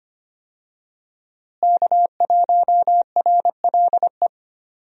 Morse code, heard as a keyed tone, about 25 wpm, 700 Hz. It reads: K1RLE
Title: USA Call Signs - 5 Characters - Repeated 3x - 25wpm